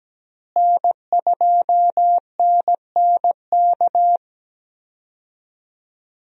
Morse code, heard as a keyed tone, about 17 wpm, 700 Hz. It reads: N2NNK
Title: USA Call Signs - 5 Characters - Repeated 2x - 17wpm